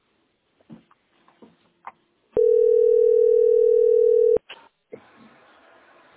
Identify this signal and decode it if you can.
Ringback tone (North America), 440 + 480 Hz.
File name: ddabfe30-a55b-4aff-83b7